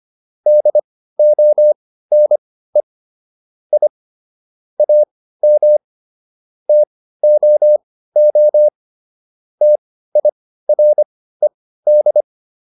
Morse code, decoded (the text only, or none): DONE I AM TOO TIRED